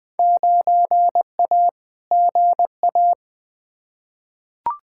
Morse code, heard as a keyed tone, about 21 wpm, 700 Hz. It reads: 9A GA
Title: Field Day Exchanges Encore - 20wpm